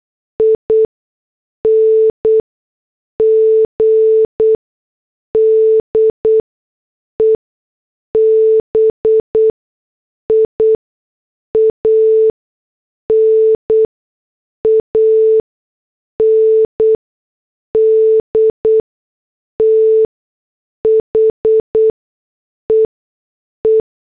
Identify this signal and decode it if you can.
Morse code, heard as a keyed tone, about 8 wpm, spaced as for 6 wpm, 440 Hz. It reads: ING DEBIAN AND THE E
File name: morse / headlines